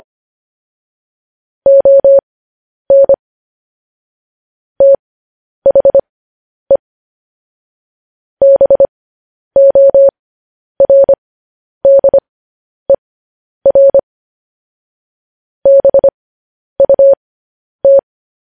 Morse code, heard as a keyed tone, about 25 wpm, spaced as for 10 wpm, 550 Hz. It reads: ON THE BORDER BUT
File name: podcast / News headlines-1x-25x10